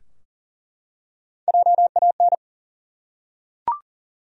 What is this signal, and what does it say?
Morse code, heard as a keyed tone, about 40 wpm, 700 Hz. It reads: JAN
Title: USA Common Names - Max 4 Letters - 40wpm